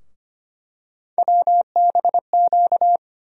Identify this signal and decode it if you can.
Morse code, heard as a keyed tone, about 25 wpm, 700 Hz. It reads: WBQ
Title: USA Call Sign Postfixes - 3 Letters 25wpm